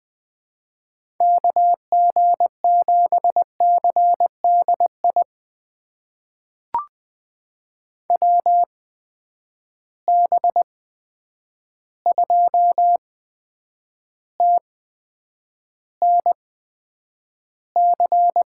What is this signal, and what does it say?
Morse code, heard as a keyed tone, about 20 wpm, spaced as for 9 wpm, 700 Hz. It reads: KG7CDI WB2TNC
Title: USA Call Signs - 6 Characters - 8x Character Spacing - 20wpm